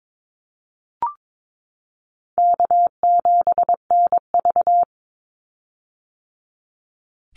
Morse code, heard as a keyed tone, about 22 wpm, 700 Hz. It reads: K7NV
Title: Easy Call Signs - 22wpm